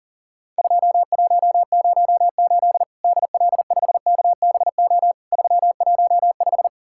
Morse code, heard as a keyed tone, about 40 wpm, 700 Hz. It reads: 2108 DL5KBO 315
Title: Worked All Europe DX Contest - QTC Exchange - 40wpm